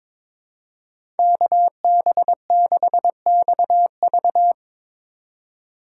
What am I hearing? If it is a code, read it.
Morse code, heard as a keyed tone, about 22 wpm, 700 Hz. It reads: KB6XV
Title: USA Call Signs - 5 Characters - 2x Character Spacing - 22wpm